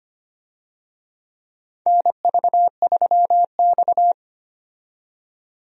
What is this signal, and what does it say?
Morse code, heard as a keyed tone, about 25 wpm, 700 Hz. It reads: NV3X
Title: USA Call Signs - 4 Characters - 25wpm